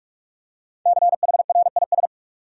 Morse code, heard as a keyed tone, about 46 wpm, 700 Hz. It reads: CHRIS
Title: USA Common Names - 45wpm